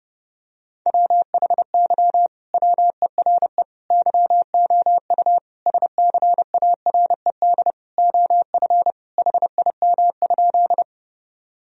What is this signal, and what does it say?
Morse code, heard as a keyed tone, about 32 wpm, 700 Hz. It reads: WHY WERE YOU SCARED OF HIM?
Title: Sentences from Top 1000 Words - Review Part 4 - 30wpm